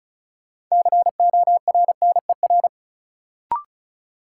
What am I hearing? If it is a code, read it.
Morse code, heard as a keyed tone, about 36 wpm, 700 Hz. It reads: CORNER
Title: Sets of 5 Words - New Word Drill - 35wpm